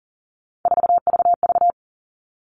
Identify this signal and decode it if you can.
Morse code, heard as a keyed tone, about 40 wpm, 700 Hz. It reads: 4VV
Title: All Cut Numbers - 3-Digit 40wpm